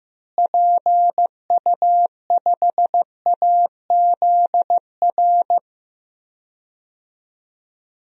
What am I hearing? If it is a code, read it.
Morse code, heard as a keyed tone, about 15 wpm, 700 Hz. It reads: PU5AZR
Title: DXCC-100 Call Signs - 6 Characters - Repeated 3x - 15wpm